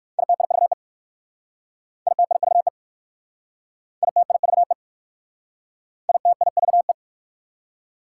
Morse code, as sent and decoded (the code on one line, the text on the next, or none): .. - .. ...- . / .. - .. ...- . / .. - .. ...- . / .. - .. ...- .
ITIVE ITIVE ITIVE ITIVE